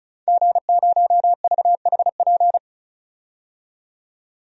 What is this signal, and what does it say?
Morse code, heard as a keyed tone, about 36 wpm, 700 Hz. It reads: G0VHP
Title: DXCC-100 Call Signs - 5 Characters - 35wpm